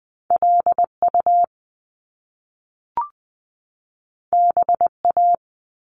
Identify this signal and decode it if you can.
Morse code, heard as a keyed tone, about 20 wpm, 700 Hz. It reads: LU BA
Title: Top Two Letter Combinations in Words 20wpm